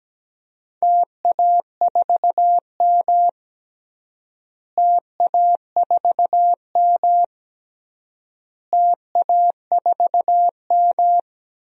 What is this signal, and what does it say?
Morse code, heard as a keyed tone, about 17 wpm, 700 Hz. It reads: TA4M TA4M TA4M
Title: DXCC-100 Call Signs - 4 Characters - Repeated 3x - 17wpm